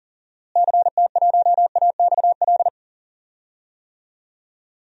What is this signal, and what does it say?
Morse code, heard as a keyed tone, about 40 wpm, 700 Hz. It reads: CT1AXL
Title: DXCC-100 Call Signs - 6 Characters - Repeated 3x - 40wpm